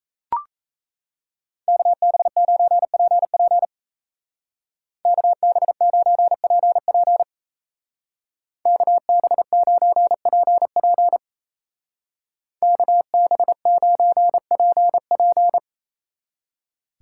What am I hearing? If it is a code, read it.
Morse code, heard as a keyed tone, about 35 wpm, 700 Hz. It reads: KB9PP KB9PP KB9PP KB9PP
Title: Speed Racing - USA 2x2 Call Signs 28wpm